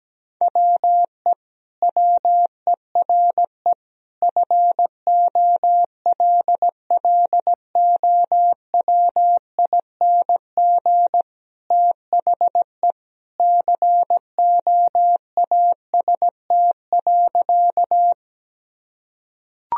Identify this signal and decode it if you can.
Morse code, heard as a keyed tone, about 17 wpm, 700 Hz. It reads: WE WERE FOLLOWING THE COAST.
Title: Sentences from Top 1000 Words - Review Part 4 - 17wpm